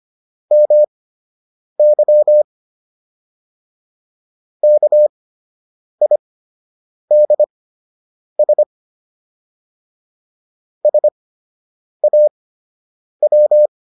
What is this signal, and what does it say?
Morse code, heard as a keyed tone, about 25 wpm, spaced as for 8 wpm, 600 Hz. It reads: MY KIDS SAW